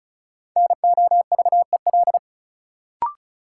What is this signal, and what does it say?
Morse code, heard as a keyed tone, about 36 wpm, 700 Hz. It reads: NOVEL
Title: Sentences from Top 1300 Words - New Word Drill - 35wpm